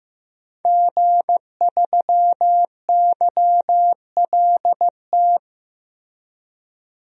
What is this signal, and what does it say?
Morse code, heard as a keyed tone, about 15 wpm, 700 Hz. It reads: G3YLT
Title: DXCC-100 Call Signs - 5 Characters - Repeated 2x - 15wpm